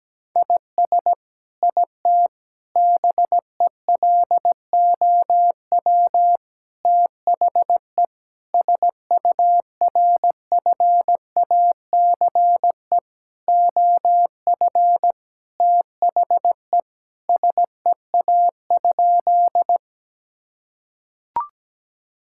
Morse code, heard as a keyed tone, about 17 wpm, 700 Hz. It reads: IS IT BELOW THE SURFACE OF THE SEA?
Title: Sentences from Top 1000 Words - Review Part 5 - 17wpm